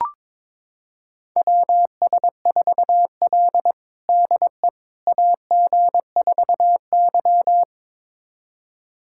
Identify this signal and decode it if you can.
Morse code, heard as a keyed tone, about 22 wpm, 700 Hz. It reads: WS4L DE AG4Y
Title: USA Callsign DE Callsign - 4 Characters - 22wpm